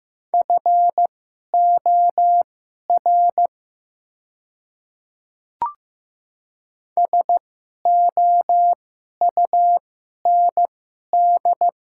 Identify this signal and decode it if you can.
Morse code, heard as a keyed tone, about 15 wpm, spaced as for 11 wpm, 690 Hz. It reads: FOR SOUND
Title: 2x Character Spacing - Top 300 Words - 15wpm